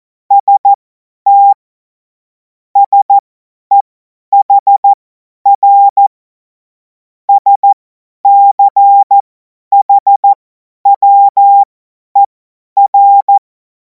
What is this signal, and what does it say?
Morse code, heard as a keyed tone, about 14 wpm, spaced as for 10 wpm, 800 Hz. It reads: ST SEHR SCHWER
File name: Morsekurs / Witze / Witz 17. Februar 2026 50 F70